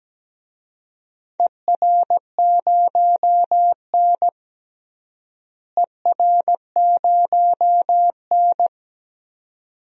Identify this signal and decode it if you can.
Morse code, heard as a keyed tone, about 17 wpm, 700 Hz. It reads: ER0N ER0N
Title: DXCC-100 Call Signs - 4 Characters - Repeated 2x - 17wpm